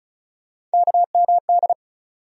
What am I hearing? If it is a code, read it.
Morse code, heard as a keyed tone, about 34 wpm, 700 Hz. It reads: KMD
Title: USA Call Sign Postfixes - 3 Letters 35wpm